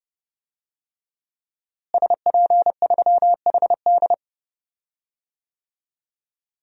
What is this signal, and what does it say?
Morse code, heard as a keyed tone, about 30 wpm, 700 Hz. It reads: SP3HD
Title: DXCC-100 Call Signs - 5 Characters - Repeated 3x - 30wpm